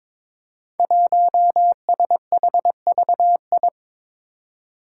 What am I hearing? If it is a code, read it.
Morse code, heard as a keyed tone, about 22 wpm, 700 Hz. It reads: 1SHVI
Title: Five Challenging Letter-Number Combinations 22wpm